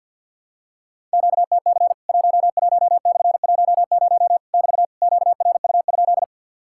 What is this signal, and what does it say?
Morse code, heard as a keyed tone, about 50 wpm, 700 Hz. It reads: QTC 11/10 <BT> QRU?
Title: Worked All Europe DX Contest - QTC Exchange - 50wpm